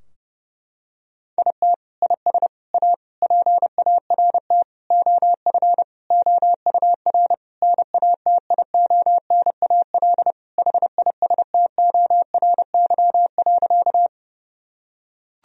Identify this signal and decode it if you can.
Morse code, heard as a keyed tone, about 32 wpm, 700 Hz. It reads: IT IS A PART OF OUR NATIONAL HISTORY.
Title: Sentences from Top 1000 Words - Review Part 4 - 30wpm